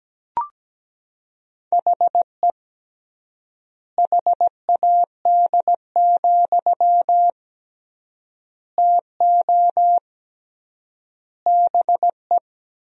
Morse code, heard as a keyed tone, about 17 wpm, 700 Hz. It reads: HE HAD, TO BE
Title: 3x Word-Spacing - Sentences from Top 500 Words - 17wpm